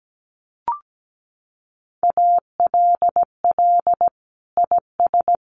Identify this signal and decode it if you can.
Morse code, heard as a keyed tone, about 17 wpm, 700 Hz. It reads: ALL IS
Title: Sentences from Top 600 Words 17wpm